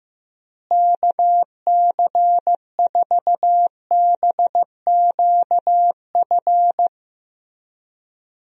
Morse code, heard as a keyed tone, about 15 wpm, 700 Hz. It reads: KC4BQF
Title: USA Call Signs - 6 Characters - Repeated 3x - 15wpm